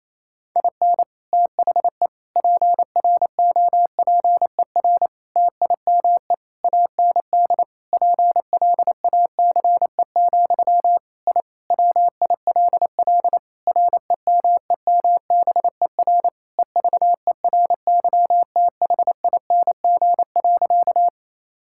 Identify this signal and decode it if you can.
Morse code, heard as a keyed tone, about 28 wpm, 700 Hz. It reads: IN THE PROPER TIME AND PLACE, I WILL REMEMBER EVERYTHING.
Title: Sentences from Top 1000 Words - Review Part 5 - 28wpm